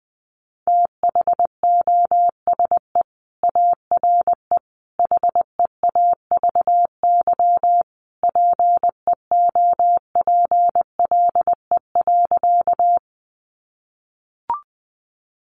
Morse code, heard as a keyed tone, about 20 wpm, 700 Hz. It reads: THOSE ARE HEAVY PEOPLE.
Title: Sentences from Top 1100 Words 20wpm